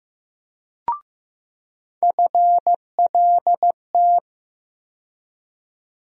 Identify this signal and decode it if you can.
Morse code, heard as a keyed tone, about 15 wpm, 700 Hz. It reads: FLT
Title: USA Call Sign Postfixes - 3 Letters 15wpm